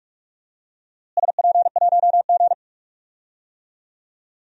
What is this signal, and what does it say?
Morse code, heard as a keyed tone, about 41 wpm, 700 Hz. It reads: SP1G
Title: DXCC-100 Call Signs - 4 Characters - Repeated 2x - 45wpm